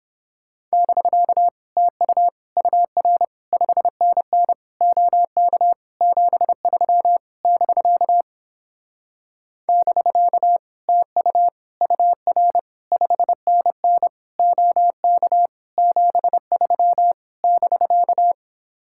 Morse code, heard as a keyed tone, about 29 wpm, 700 Hz. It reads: <BK> TU UR 5NN OK 73 <BK> <BK> TU UR 5NN OK 73 <BK>